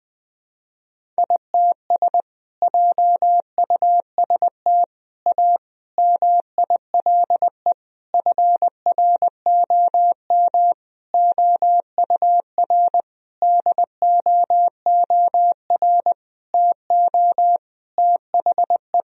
Morse code, heard as a keyed tone, about 20 wpm, 700 Hz. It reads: ITS JUST A MILE FROM OUR DOOR TO THE
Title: Sentences from Top 1000 Words - Review Part 1 - 20wpm